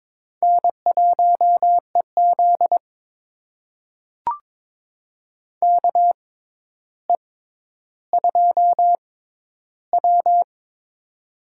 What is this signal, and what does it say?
Morse code, heard as a keyed tone, about 21 wpm, spaced as for 11 wpm, 700 Hz. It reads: N1EZ KE2W
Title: USA Call Signs - 4 Characters - 6x Character Spacing - 22wpm